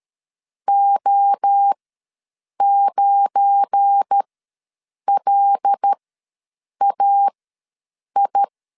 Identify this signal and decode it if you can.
Morse code, heard as a keyed tone, about 13 wpm, spaced as for 7 wpm, 800 Hz. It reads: O9LAI
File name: LEZIONI / 7 GF9Z / 7 GF9 ws 12wpm 3344